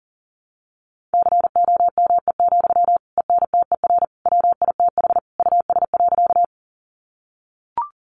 Morse code, heard as a keyed tone, about 40 wpm, 700 Hz. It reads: COME, ENTER WITH US.
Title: Sentences from Top 1300 Words 40wpm